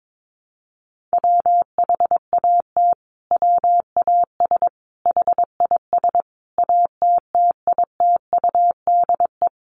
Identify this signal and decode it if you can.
Morse code, heard as a keyed tone, about 22 wpm, 700 Hz. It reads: WHAT WAS HIS ATTITUDE
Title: Sentences from Top 1300 Words 22wpm